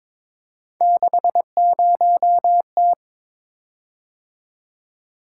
Morse code, heard as a keyed tone, about 22 wpm, 700 Hz. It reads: 60T
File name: practice / 22wpm / Three Letter-Number Combinations - 22wpm